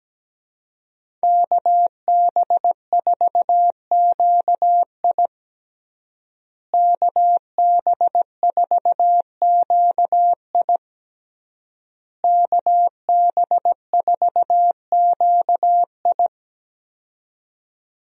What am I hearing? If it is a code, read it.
Morse code, heard as a keyed tone, about 17 wpm, 700 Hz. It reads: KB4QI KB4QI KB4QI
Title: USA Call Signs - 5 Characters - Repeated 3x - 17wpm